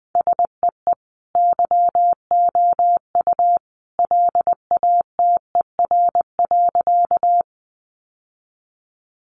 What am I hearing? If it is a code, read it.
Morse code, heard as a keyed tone, about 20 wpm, 700 Hz. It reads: SEE YOU LATER.
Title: Sentences from Top 700 Words 20wpm